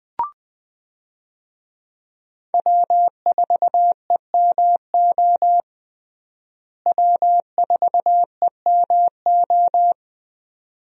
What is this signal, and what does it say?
Morse code, heard as a keyed tone, about 20 wpm, 700 Hz. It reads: W4EMO W4EMO
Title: USA Call Signs - 5 Characters - Repeated 2x - 20wpm